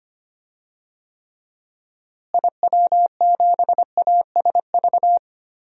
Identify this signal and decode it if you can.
Morse code, heard as a keyed tone, about 25 wpm, 700 Hz. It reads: IW7ASV
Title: DXCC-100 Call Signs - 6 Characters - Repeated 2x - 25wpm